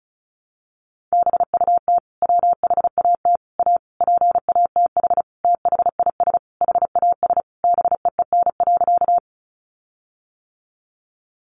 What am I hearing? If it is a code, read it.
Morse code, heard as a keyed tone, about 34 wpm, 700 Hz. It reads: BUT WHAT A PATH THIS HAS BEEN.
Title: Sentences from Top 1400 Words 35wpm